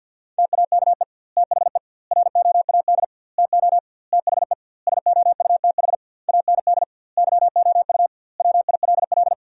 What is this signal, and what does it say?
Morse code, heard as a keyed tone, about 50 wpm, 700 Hz. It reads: TAKE THE ROAD TO THE SOUTH AND YOU WILL
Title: 2x Word-Spacing - Sentences from Top 400 Words - 50wpm